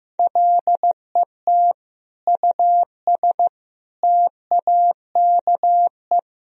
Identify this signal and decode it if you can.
Morse code, heard as a keyed tone, about 15 wpm, 700 Hz. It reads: LET US TAKE
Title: Sentences from Top 1000 Words - Review Part 1 - 15wpm